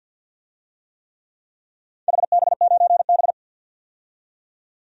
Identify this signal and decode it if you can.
Morse code, heard as a keyed tone, about 51 wpm, 700 Hz. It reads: HB9B
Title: DXCC-100 Call Signs - 4 Characters - 50wpm